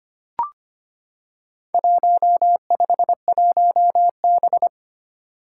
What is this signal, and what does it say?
Morse code, heard as a keyed tone, about 25 wpm, 700 Hz. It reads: 151B
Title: Four Challenging Letter-Number Combinations 25wpm